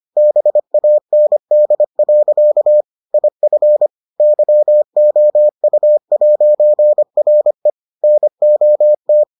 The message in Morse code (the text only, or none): BAND. IF YOU'RE NOT